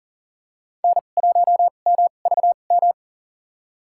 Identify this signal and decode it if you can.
Morse code, heard as a keyed tone, about 40 wpm, spaced as for 29 wpm, 700 Hz. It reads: N1MVM
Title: USA Call Signs - 5 Characters - 2x Character Spacing - 40wpm